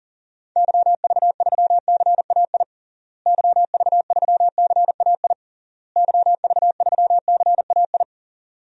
Morse code, heard as a keyed tone, about 40 wpm, 700 Hz. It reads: YV3CAI YV3CAI YV3CAI
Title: DXCC-100 Call Signs - 6 Characters - Repeated 3x - 40wpm